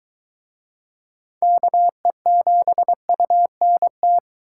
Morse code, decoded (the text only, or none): KE7UNT